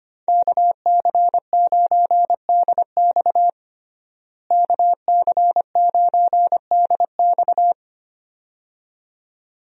Morse code, heard as a keyed tone, about 25 wpm, 690 Hz. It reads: KC9DX KC9DX
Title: USA Call Signs - 5 Characters - Repeated 2x - 25wpm